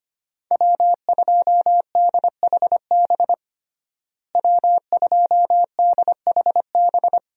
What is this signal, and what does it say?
Morse code, heard as a keyed tone, about 25 wpm, 700 Hz. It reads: W2DHB W2DHB
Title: USA Call Signs - 5 Characters - Repeated 3x - 25wpm